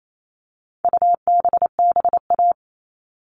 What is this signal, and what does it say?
Morse code, heard as a keyed tone, about 28 wpm, 700 Hz. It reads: UBBA
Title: All Cut Numbers - 4-Digit 28wpm